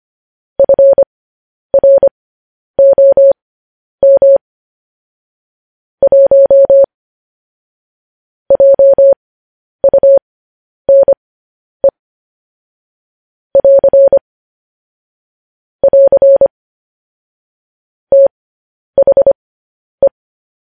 Morse code, heard as a keyed tone, about 25 wpm, spaced as for 10 wpm, 550 Hz. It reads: FROM 1 JUNE <AR> <AR> THE
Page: news-headlines-in-morse-code-at-10-wpm